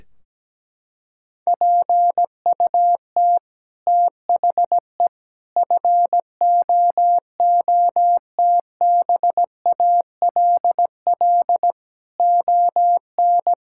Morse code, heard as a keyed tone, about 17 wpm, 700 Hz. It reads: PUT THE FOOTBALL ON
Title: Sentences from Top 1600 Words 17wpm